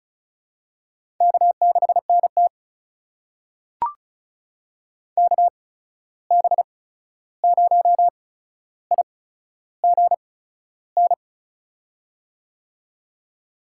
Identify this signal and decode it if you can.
Morse code, heard as a keyed tone, about 34 wpm, spaced as for 13 wpm, 700 Hz. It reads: K6NT KB0IGN
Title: USA Call Signs - 8x Character Spacing - 35wpm